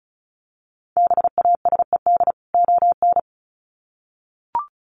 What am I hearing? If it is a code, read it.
Morse code, heard as a keyed tone, about 34 wpm, 700 Hz. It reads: BASED ON
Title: Sets of 2 Words 35wpm